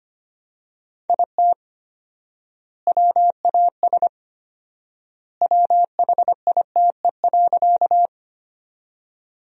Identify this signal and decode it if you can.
Morse code, heard as a keyed tone, about 25 wpm, 700 Hz. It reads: IT WAS WHITE.
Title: 4x Word-Spacing - Sentences from Top 300 Words - 25wpm